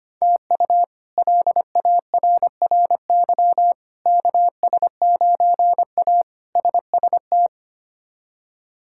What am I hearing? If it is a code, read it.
Morse code, heard as a keyed tone, about 25 wpm, 700 Hz. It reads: TU LARRY KS9A SST
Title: K1USN SST QSO Practice - 25wpm